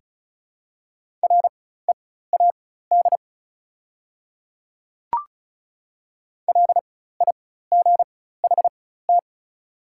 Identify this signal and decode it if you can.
Morse code, heard as a keyed tone, about 36 wpm, spaced as for 16 wpm, 700 Hz. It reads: READ LIGHT
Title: 4x Character Spacing - Top 200 Words - 35wpm